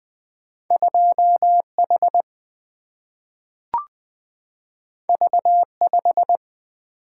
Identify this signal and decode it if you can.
Morse code, heard as a keyed tone, about 20 wpm, 700 Hz. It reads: 2H V5
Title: Single-Letter Single-Number 20wpm